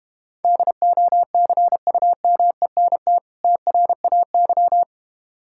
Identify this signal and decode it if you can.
Morse code, heard as a keyed tone, about 33 wpm, 700 Hz. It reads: DOCUMENT TRAY